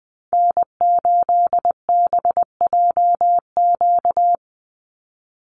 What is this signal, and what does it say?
Morse code, heard as a keyed tone, about 19 wpm, 700 Hz. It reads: N8BJQ
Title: Call Signs - 20wpm